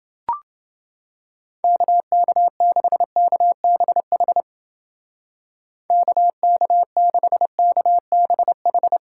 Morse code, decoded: KK6KBH KK6KBH

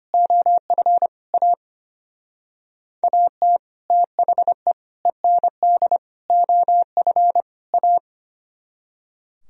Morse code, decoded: OF A AT THE END OF A

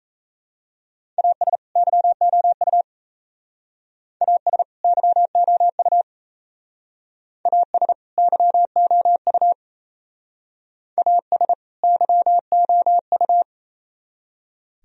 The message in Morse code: AS YOU AS YOU AS YOU AS YOU

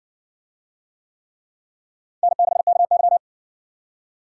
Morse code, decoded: N6KY

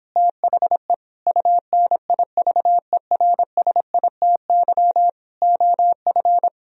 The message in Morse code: THE UNIVERSITY OF

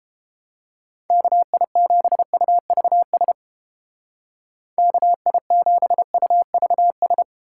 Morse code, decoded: KI7UVS KI7UVS

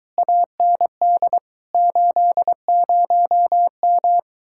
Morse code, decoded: AND 80M

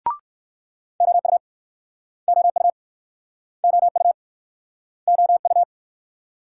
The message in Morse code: OU OU OU OU

